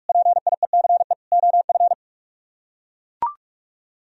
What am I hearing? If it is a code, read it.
Morse code, heard as a keyed tone, about 40 wpm, 690 Hz. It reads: PIECE OF